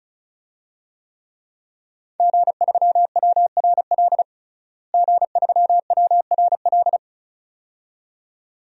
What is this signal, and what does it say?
Morse code, heard as a keyed tone, about 36 wpm, 700 Hz. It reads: G3WRL G3WRL
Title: DXCC-100 Call Signs - 5 Characters - Repeated 2x - 35wpm